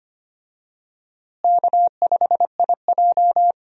Morse code, heard as a keyed tone, about 25 wpm, 700 Hz. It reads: K5IJ
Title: USA Call Signs - 4 Characters - Repeated 2x - 25wpm